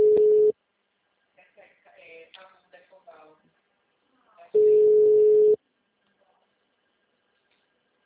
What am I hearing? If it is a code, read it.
Ringback tone (Europe, 425 Hz), with speech.